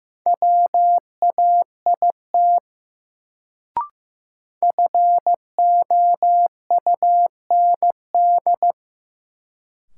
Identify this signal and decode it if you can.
Morse code, heard as a keyed tone, about 15 wpm, 700 Hz. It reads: WAIT FOUND
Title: Sets of 5 Words Encore - New Word Drill - 15wpm